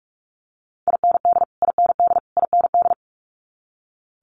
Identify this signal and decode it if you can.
Morse code, heard as a keyed tone, about 46 wpm, 700 Hz. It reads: IND IND IND
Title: Multi-letter Phonemes Repeated 3x 45wpm